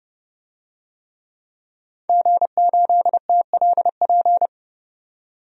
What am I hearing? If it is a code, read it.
Morse code, heard as a keyed tone, about 31 wpm, 700 Hz. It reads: G8TLP
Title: DXCC-100 Call Signs - 5 Characters - 30wpm